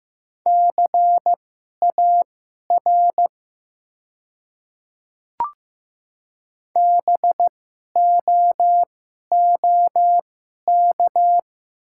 Morse code, heard as a keyed tone, about 15 wpm, spaced as for 11 wpm, 700 Hz. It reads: CAR BOOK
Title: 2x Character Spacing - Top 500 Words - 15wpm